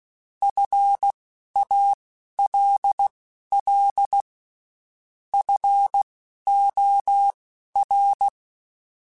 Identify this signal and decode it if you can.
Morse code, heard as a keyed tone, about 16 wpm, spaced as for 11 wpm, 800 Hz. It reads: FALL FOR
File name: Quotes Spaced / Quotes Spaced-mp3s / StandFor sp @15